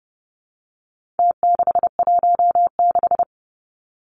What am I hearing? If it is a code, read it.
Morse code, heard as a keyed tone, about 30 wpm, 700 Hz. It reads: T616
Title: Numbers - Contest Style - 4-Digit 30wpm